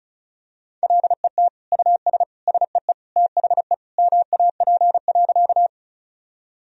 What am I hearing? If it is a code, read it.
Morse code, heard as a keyed tone, about 36 wpm, 700 Hz. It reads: LET US SEE THE MAP.
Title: Sentences from Top 1000 Words - Review Part 2 - 35wpm